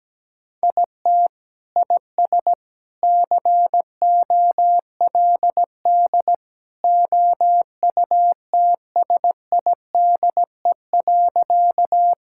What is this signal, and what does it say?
Morse code, heard as a keyed tone, about 17 wpm, 700 Hz. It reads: IT IS COLD OUTSIDE.
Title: Sentences from Top 1000 Words - Review Part 6 - 17wpm